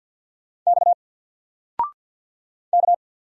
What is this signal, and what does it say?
Morse code, heard as a keyed tone, about 48 wpm, 700 Hz. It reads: X K
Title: Single Character (Letter:Number:Punctuation:Prosign) 50wpm